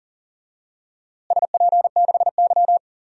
Morse code, heard as a keyed tone, about 40 wpm, 700 Hz. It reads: SP6Y
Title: DXCC-100 Call Signs - 4 Characters - Repeated 2x - 40wpm